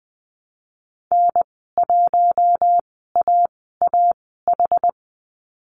Morse code, heard as a keyed tone, about 20 wpm, spaced as for 14 wpm, 700 Hz. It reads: N1AAH
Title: 161 Lesson 26g - Letter g - Call Signs 20wpm